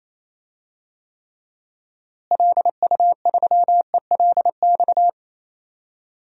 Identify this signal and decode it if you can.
Morse code, heard as a keyed tone, about 28 wpm, 700 Hz. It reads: LU3ELX
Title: DXCC-100 Call Signs - 6 Characters - Repeated 3x - 28wpm